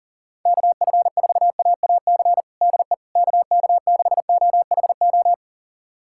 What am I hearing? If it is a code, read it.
Morse code, heard as a keyed tone, about 37 wpm, 700 Hz. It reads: KF4AAC DE KK6OHO
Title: USA Callsign DE Callsign 40wpm